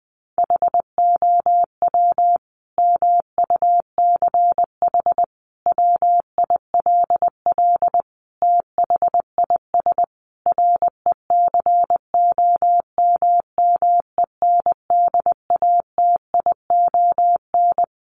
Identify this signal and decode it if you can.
Morse code, heard as a keyed tone, about 20 wpm, 700 Hz. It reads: HOW MUCH WILL THIS RECOMMENDATION
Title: Sentences from Top 1400 Words 20wpm